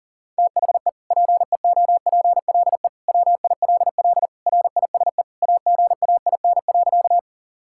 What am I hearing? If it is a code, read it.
Morse code, heard as a keyed tone, about 37 wpm, 700 Hz. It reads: THE PEOPLE WILL RISE AGAIN.
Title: Sentences from Top 1000 Words - Review Part 6 - 40wpm